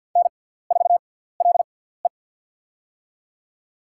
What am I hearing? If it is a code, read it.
Morse code, heard as a keyed tone, about 51 wpm, spaced as for 17 wpm, 700 Hz. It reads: N4LE
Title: USA Call Signs - 4 Characters - 6x Character Spacing - 50wpm